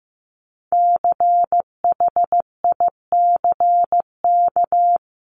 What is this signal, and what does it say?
Morse code, heard as a keyed tone, about 15 wpm, 700 Hz. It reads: CHICK
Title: CWT Exchange 15wpm